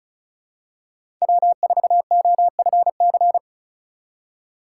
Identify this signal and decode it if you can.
Morse code, heard as a keyed tone, about 34 wpm, 700 Hz. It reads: W4OFC
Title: USA Call Signs - 5 Characters - 4x Character Spacing - 35wpm